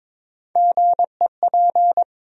Morse code, heard as a keyed tone, about 22 wpm, 700 Hz. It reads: GEP
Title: USA Call Sign Postfixes - 3 Letters 22wpm